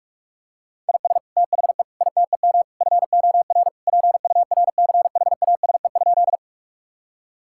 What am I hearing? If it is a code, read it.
Morse code, heard as a keyed tone, about 46 wpm, 700 Hz. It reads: IS THE ITEM FOR PURCHASE?
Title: Sentences from Top 1000 Words - Review Part 6 - 45wpm